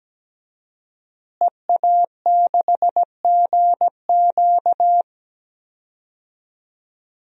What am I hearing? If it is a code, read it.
Morse code, heard as a keyed tone, about 17 wpm, 700 Hz. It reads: EA6GQ
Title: DXCC-100 Call Signs - 5 Characters - Repeated 2x - 17wpm